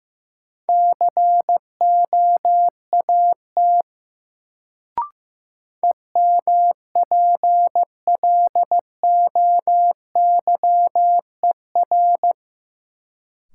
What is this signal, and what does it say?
Morse code, heard as a keyed tone, about 15 wpm, 700 Hz. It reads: COAT EMPLOYER
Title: Sentences from Top 1600 Words - New Word Drill - 15wpm